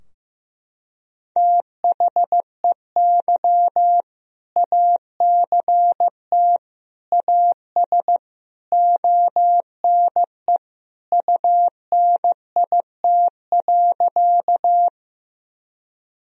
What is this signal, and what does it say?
Morse code, heard as a keyed tone, about 15 wpm, 700 Hz. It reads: THEY ACT AS ONE UNIT.
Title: Sentences from Top 1000 Words - Review Part 2 - 15wpm